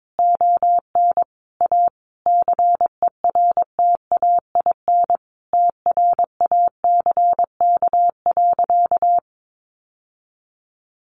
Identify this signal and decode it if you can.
Morse code, heard as a keyed tone, about 22 wpm, 700 Hz. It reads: ON A CERTAIN TRACK.
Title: Sentences from Top 1100 Words 22wpm